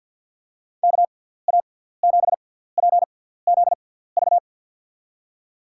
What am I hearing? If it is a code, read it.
Morse code, heard as a keyed tone, about 54 wpm, spaced as for 17 wpm, 700 Hz. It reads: KA7PZV
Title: USA Call Signs - 6 Characters - 6x Character Spacing - 50wpm